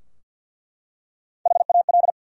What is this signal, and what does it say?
Morse code, heard as a keyed tone, about 48 wpm, 700 Hz. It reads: HAL
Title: USA Common Names - Max 4 Letters - 50wpm